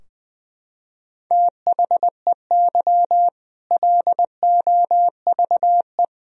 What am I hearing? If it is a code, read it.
Morse code, heard as a keyed tone, about 20 wpm, 700 Hz. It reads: THEY LOVE
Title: Sentences from Top 1000 Words - Review Part 3 - 20wpm